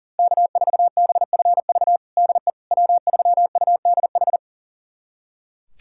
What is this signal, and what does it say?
Morse code, heard as a keyed tone, about 40 wpm, 700 Hz. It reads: K4BFV DE W3UDH
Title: USA Callsign DE Callsign - 5 Characters - 40wpm